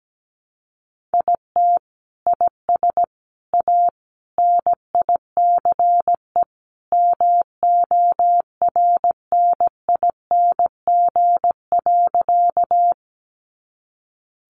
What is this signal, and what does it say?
Morse code, heard as a keyed tone, about 17 wpm, 700 Hz. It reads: IT IS A NICE MORNING.
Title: Sentences from Top 1100 Words 17wpm